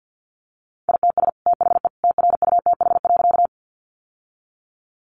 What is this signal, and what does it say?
Morse code, heard as a keyed tone, about 50 wpm, 700 Hz. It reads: ITS THE TRUTH.
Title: Sentences from Top 800 Words 50wpm